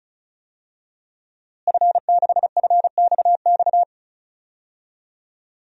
Morse code, heard as a keyed tone, about 36 wpm, 700 Hz. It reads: F6FXX
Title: DXCC-100 Call Signs - 5 Characters - Repeated 3x - 35wpm